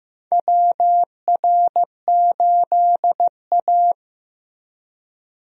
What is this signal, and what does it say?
Morse code, heard as a keyed tone, about 16 wpm, 700 Hz. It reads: WR8A
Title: USA Call Signs - 4 Characters - Repeated 2x - 15wpm